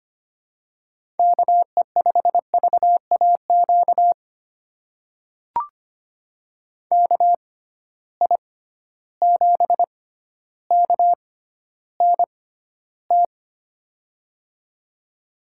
Morse code, heard as a keyed tone, about 25 wpm, spaced as for 13 wpm, 700 Hz. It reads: KE5VAQ KI7KNT
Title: USA Call Signs - 6 Characters - 6x Character Spacing - 25wpm